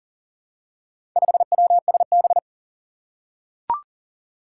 Morse code, heard as a keyed tone, about 41 wpm, 700 Hz. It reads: 5WSB
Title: Four Challenging Letter-Number Combinations 40wpm